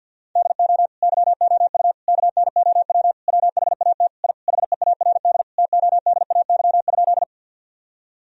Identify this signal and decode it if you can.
Morse code, heard as a keyed tone, about 50 wpm, 700 Hz. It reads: DO YOU KNOW WHAT I HEARD TODAY?
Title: Sentences from Top 1000 Words - Review Part 4 - 50wpm